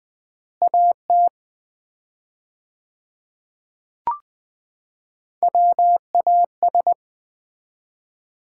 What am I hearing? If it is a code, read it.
Morse code, heard as a keyed tone, about 21 wpm, 700 Hz. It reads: AT WAS
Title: 4x Word-Spacing - Sets of 4 Words - 20wpm